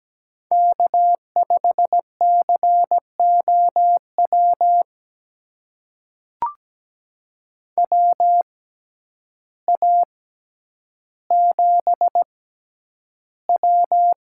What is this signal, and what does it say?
Morse code, heard as a keyed tone, about 17 wpm, spaced as for 9 wpm, 700 Hz. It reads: K5COW WA7W
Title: USA Call Signs - 5 Characters - 6x Character Spacing - 17wpm